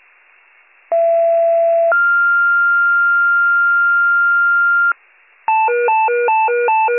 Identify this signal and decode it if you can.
Two-tone page, heard as a one-second tone, then a three-second tone, about 670 then 1400 Hz. It is heard from 0.9 to 4.9 s.